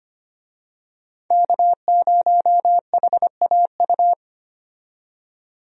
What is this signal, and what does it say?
Morse code, heard as a keyed tone, about 25 wpm, 700 Hz. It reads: K0HAU
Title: USA Call Signs - 5 Characters - 25wpm